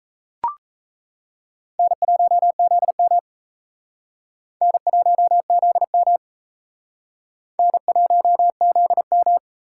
Morse code, heard as a keyed tone, about 38 wpm, 700 Hz. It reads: N1ZM N1ZM N1ZM